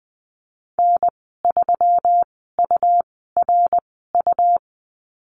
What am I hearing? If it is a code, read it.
Morse code, heard as a keyed tone, about 20 wpm, spaced as for 14 wpm, 700 Hz. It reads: N3URU
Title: 140 Lesson 23g - Letter f - Call Signs 20wpm